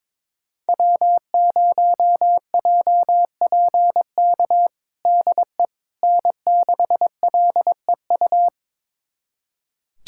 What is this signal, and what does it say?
Morse code, heard as a keyed tone, about 22 wpm, 700 Hz. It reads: W0JPK DE N6LEU
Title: USA Callsign DE Callsign - 5 Characters - 22wpm